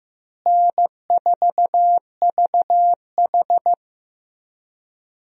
Morse code, heard as a keyed tone, about 15 wpm, 700 Hz. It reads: N4VH
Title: USA Call Signs - 4 Characters - Repeated 3x - 15wpm